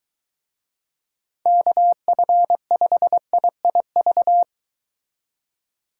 Morse code, heard as a keyed tone, about 23 wpm, 700 Hz. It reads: KF5IIV